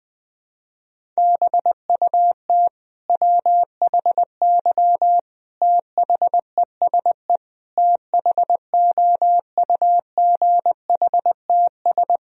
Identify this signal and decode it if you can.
Morse code, heard as a keyed tone, about 20 wpm, 700 Hz. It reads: BUT WHY THESE THOUGHTS
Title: Sentences from Top 1000 Words - Review Part 1 - 20wpm